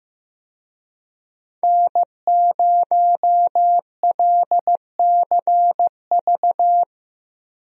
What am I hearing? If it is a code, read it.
Morse code, heard as a keyed tone, about 15 wpm, 700 Hz. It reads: N0LCV
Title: USA Call Signs - 5 Characters - Repeated 3x - 15wpm